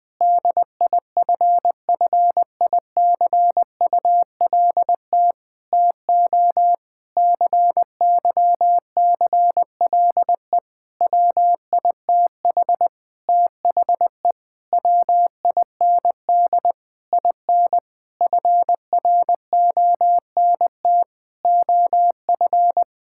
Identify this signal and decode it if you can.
Morse code, heard as a keyed tone, about 21 wpm, 700 Hz. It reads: DIFFICULT TO CYCLE WITH THE WIND IN FRONT OF
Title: Sentences from Top 1000 Words - Review Part 6 - 20wpm